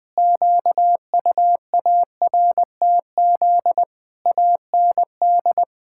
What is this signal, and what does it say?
Morse code, heard as a keyed tone, about 20 wpm, 700 Hz. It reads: QUARTZ AND
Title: Pangram Challenge v2 - 20wpm